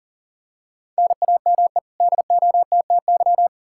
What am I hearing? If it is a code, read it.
Morse code, heard as a keyed tone, about 40 wpm, 700 Hz. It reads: NAME DOTTY
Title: K1USN SST QSO Practice 40wpm